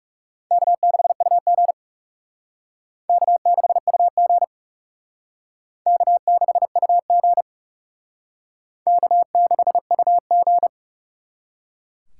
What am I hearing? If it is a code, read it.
Morse code, heard as a keyed tone, about 37 wpm, 700 Hz. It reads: K6UG K6UG K6UG K6UG
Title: Speed Racing - USA Call Signs 30wpm